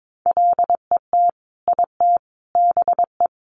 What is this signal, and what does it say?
Morse code, heard as a keyed tone, about 22 wpm, 700 Hz. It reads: LET IT BE
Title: Sentences from Top 1400 Words 22wpm